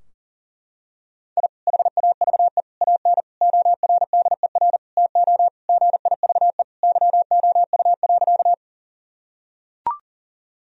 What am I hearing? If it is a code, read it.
Morse code, heard as a keyed tone, about 40 wpm, 700 Hz. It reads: I HAVE AN ORDER TO GIVE YOU.
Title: Sentences from Top 1000 Words - Review Part 2 - 40wpm